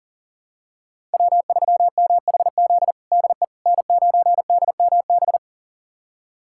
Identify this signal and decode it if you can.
Morse code, heard as a keyed tone, about 40 wpm, 700 Hz. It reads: W3MHZ DE N9DMB
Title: USA Callsign DE Callsign - 5 Characters - 40wpm